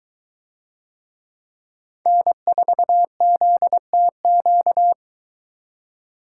Morse code, heard as a keyed tone, about 23 wpm, 700 Hz. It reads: N4ZTQ